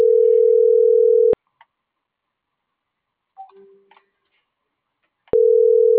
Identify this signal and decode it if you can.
Ringback tone (North America), 440 + 480 Hz.
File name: CENTRAL GENERAL / out karla.sky - .66081